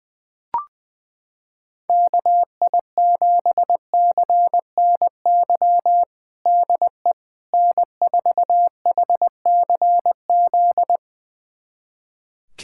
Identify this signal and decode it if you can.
Morse code, heard as a keyed tone, about 20 wpm, 700 Hz. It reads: KI7CNY DE N4HCZ
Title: USA Callsign DE Callsign 20wpm